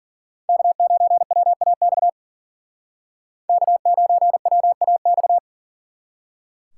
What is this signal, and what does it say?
Morse code, heard as a keyed tone, about 50 wpm, 700 Hz. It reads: K9WAX K9WAX